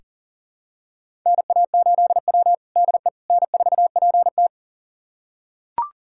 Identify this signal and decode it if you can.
Morse code, heard as a keyed tone, about 40 wpm, 700 Hz. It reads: NA8W DE N4PT
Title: USA Callsign DE Callsign - 4 Characters - 40wpm